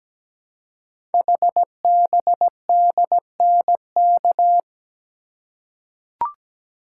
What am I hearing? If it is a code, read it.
Morse code, heard as a keyed tone, about 17 wpm, 700 Hz. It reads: HBDNK
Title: Code Groups - 5 Letters - 17wpm